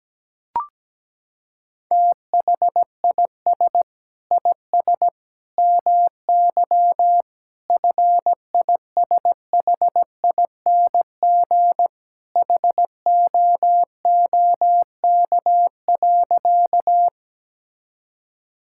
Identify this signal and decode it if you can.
Morse code, heard as a keyed tone, about 17 wpm, 700 Hz. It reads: THIS IS MY FISHING HOOK.
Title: Sentences from Top 1600 Words 17wpm